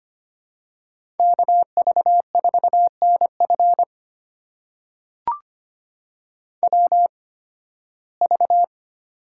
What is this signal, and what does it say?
Morse code, heard as a keyed tone, about 25 wpm, spaced as for 16 wpm, 700 Hz. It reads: KV4NF WV
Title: USA Call Signs - 8x Character Spacing - 25wpm